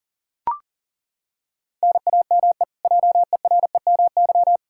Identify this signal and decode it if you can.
Morse code, heard as a keyed tone, about 40 wpm, 700 Hz. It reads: NAME JEREMY
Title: K1USN SST QSO Practice 40wpm